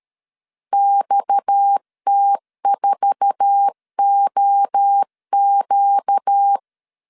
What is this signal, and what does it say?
Morse code, heard as a keyed tone, about 13 wpm, 800 Hz. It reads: XT4OQ
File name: CW / DET / 1 9 1500C S A 12wpm 3800